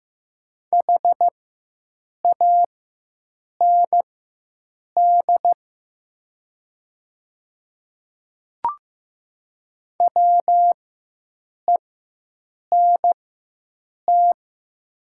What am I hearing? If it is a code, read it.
Morse code, heard as a keyed tone, about 15 wpm, spaced as for 7 wpm, 700 Hz. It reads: HAND WENT
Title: 4x Character Spacing - Top 300 Words - 15wpm